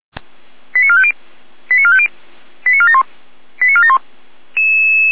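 Five-tone selective call (ZVEI), heard as quick runs of five tones, tones of about 70 ms. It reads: 89450 89450 89471 89471
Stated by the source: ZVEI five-tone selective call received on an SDR.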